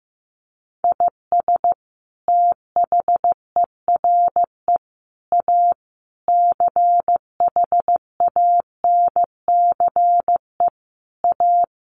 Morse code, heard as a keyed tone, about 15 wpm, 700 Hz. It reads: IS THERE A CHANCE A
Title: Sentences from Top 900 Words 15wpm